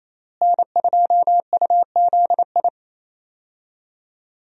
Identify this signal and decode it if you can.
Morse code, heard as a keyed tone, about 28 wpm, 700 Hz. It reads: N2UZI
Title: USA Call Signs - 5 Characters - Repeated 2x - 28wpm